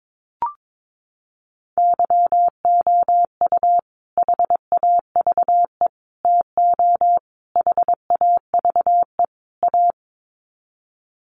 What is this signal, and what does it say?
Morse code, heard as a keyed tone, about 22 wpm, 700 Hz. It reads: YOU HAVE TO HAVE A
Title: Sets of 5 Words Encore 22wpm